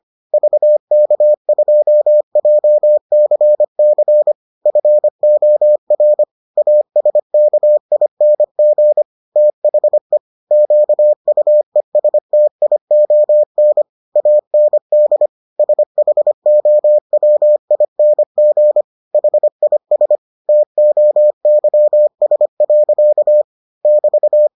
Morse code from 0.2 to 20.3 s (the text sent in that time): VK2JCC FOR ASKING THE QUESTION AND SHOWING HIS